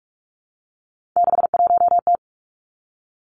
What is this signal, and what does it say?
Morse code, heard as a keyed tone, about 46 wpm, 700 Hz. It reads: 61T
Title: Three Challenging Letter-Number Combinations 45wpm